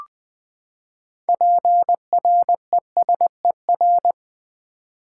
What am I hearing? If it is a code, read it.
Morse code, heard as a keyed tone, about 20 wpm, 700 Hz. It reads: PRESER
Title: Build A Word - 6 Letters At A Time - 12-Letter Words - 20wpm